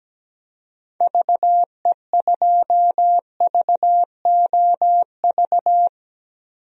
Morse code, heard as a keyed tone, about 17 wpm, 700 Hz. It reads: VE2VOV
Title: DXCC-100 Call Signs - 6 Characters - Repeated 3x - 17wpm